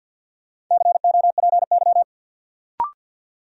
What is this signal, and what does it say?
Morse code, heard as a keyed tone, about 51 wpm, 700 Hz. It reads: COPY
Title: Sentences from Top 1400 Words - New Word Drill - 50wpm